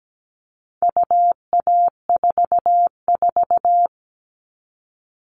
Morse code, heard as a keyed tone, about 17 wpm, 700 Hz. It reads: UA44
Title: All Cut Numbers - 4-Digit 17wpm